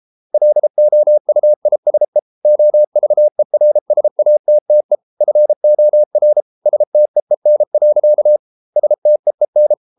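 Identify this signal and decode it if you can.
Morse code, heard as a keyed tone, about 33 wpm, 600 Hz. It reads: LOUISE OVERSATTE FOR STEEN. STEEN
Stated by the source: Morse code at 33 wpm, QRQ.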